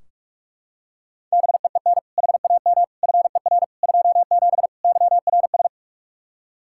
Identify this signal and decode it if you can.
Morse code, heard as a keyed tone, about 43 wpm, 700 Hz. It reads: BEEN HAM FER 27 YRS
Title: K1USN SST QSO Practice 45wpm